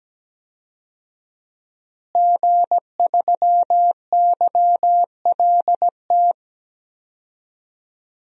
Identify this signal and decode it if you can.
Morse code, heard as a keyed tone, about 17 wpm, 700 Hz. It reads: G3YLT
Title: DXCC-100 Call Signs - 5 Characters - Repeated 2x - 17wpm